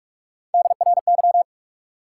Morse code, heard as a keyed tone, about 44 wpm, 700 Hz. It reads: DRY
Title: Three Letter Words - 45wpm